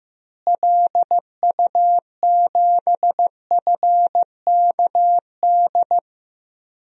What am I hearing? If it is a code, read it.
Morse code, heard as a keyed tone, about 15 wpm, 700 Hz. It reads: LU7FKD
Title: DXCC-100 Call Signs - 6 Characters - Repeated 2x - 15wpm